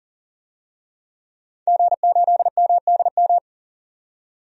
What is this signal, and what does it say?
Morse code, heard as a keyed tone, about 40 wpm, 700 Hz. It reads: G8MDM
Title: DXCC-100 Call Signs - 5 Characters - 40wpm